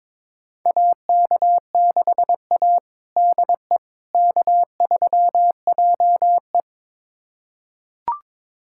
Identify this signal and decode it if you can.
Morse code, heard as a keyed tone, about 22 wpm, 700 Hz. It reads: AK6A DE K3JE
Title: USA Callsign DE Callsign - 4 Characters - 22wpm